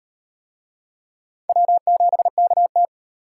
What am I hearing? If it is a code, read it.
Morse code, heard as a keyed tone, about 35 wpm, 700 Hz. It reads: W7KT